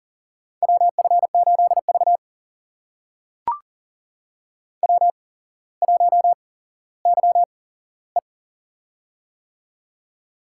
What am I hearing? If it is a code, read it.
Morse code, heard as a keyed tone, about 40 wpm, spaced as for 17 wpm, 700 Hz. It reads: WF8V W1YE
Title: USA Call Signs - 4 Characters - 8x Character Spacing - 40wpm